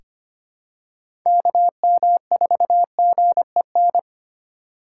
Morse code, heard as a keyed tone, about 25 wpm, 700 Hz. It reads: KM4GEN
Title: USA Call Signs - 3x Character Spacing - 25wpm